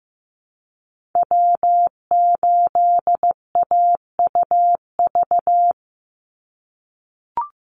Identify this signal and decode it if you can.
Morse code, heard as a keyed tone, about 15 wpm, 700 Hz. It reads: W8AUV
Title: USA Call Signs 15wpm